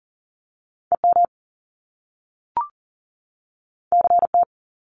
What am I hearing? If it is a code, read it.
Morse code, heard as a keyed tone, about 40 wpm, 700 Hz. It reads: EM CT
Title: Top Two Letter Combinations in Words 40wpm